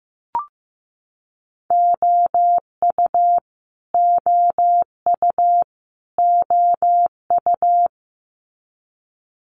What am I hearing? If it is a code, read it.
Morse code, heard as a keyed tone, about 15 wpm, 700 Hz. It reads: OU OU OU
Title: Multi-letter Phonemes Repeated 3x 15wpm